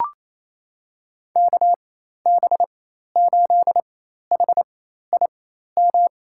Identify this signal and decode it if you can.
Morse code, heard as a keyed tone, about 28 wpm, spaced as for 13 wpm, 700 Hz. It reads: KB8HIM
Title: USA Call Signs - 4x Character Spacing - 28wpm